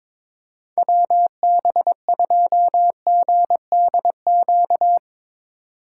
Morse code, heard as a keyed tone, about 22 wpm, 700 Hz. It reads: WB2GDQ
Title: USA Call Signs - 6 Characters - 4x Character Spacing - 22wpm